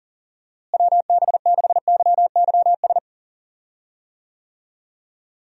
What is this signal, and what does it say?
Morse code, heard as a keyed tone, about 40 wpm, 700 Hz. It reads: WB6YYS
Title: USA Call Signs - 6 Characters - Repeated 2x - 40wpm